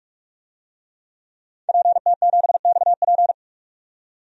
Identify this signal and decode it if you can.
Morse code, heard as a keyed tone, about 46 wpm, 700 Hz. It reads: PT7XP
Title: DXCC-100 Call Signs - 5 Characters - 45wpm